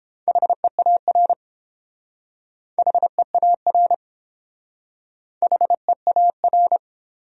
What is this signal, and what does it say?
Morse code, heard as a keyed tone, about 30 wpm, 700 Hz. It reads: HEAR HEAR HEAR